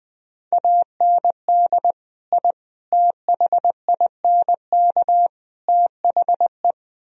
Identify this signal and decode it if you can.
Morse code, heard as a keyed tone, about 20 wpm, 700 Hz. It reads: AND I THINK THE
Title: Speed-Racing - Sets of 4 Words 17wpm